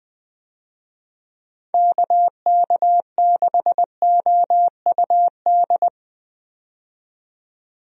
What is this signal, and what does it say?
Morse code, heard as a keyed tone, about 20 wpm, 700 Hz. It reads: KK6OUD
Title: USA Call Signs - 6 Characters - 20wpm